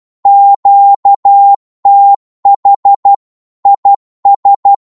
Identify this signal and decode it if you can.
Morse code, heard as a keyed tone, about 12 wpm, 800 Hz. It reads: QTH IS
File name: Morse / qso-12wpm